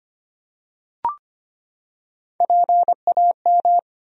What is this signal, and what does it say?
Morse code, heard as a keyed tone, about 25 wpm, 700 Hz. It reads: PAM
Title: USA Common Names - Max 3 Letters - 25wpm